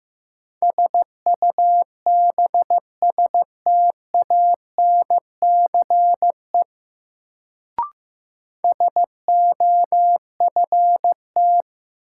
Morse code, heard as a keyed tone, about 15 wpm, 700 Hz. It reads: SUBSTANCE SOFT
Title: Sentences from Top 1500 Words - New Word Drill - 15wpm